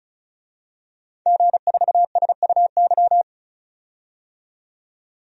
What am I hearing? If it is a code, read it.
Morse code, heard as a keyed tone, about 34 wpm, 700 Hz. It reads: G4SUY
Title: DXCC-100 Call Signs - 5 Characters - 35wpm